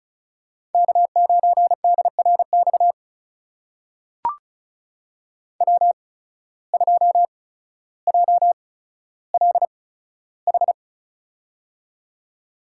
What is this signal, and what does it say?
Morse code, heard as a keyed tone, about 36 wpm, spaced as for 15 wpm, 700 Hz. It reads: K9DRX W2JLH
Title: USA Call Signs - 5 Characters - 8x Character Spacing - 35wpm